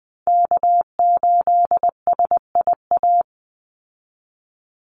Morse code, heard as a keyed tone, about 20 wpm, 700 Hz. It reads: K8SIA
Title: Call Signs Repeated 2x - 20wpm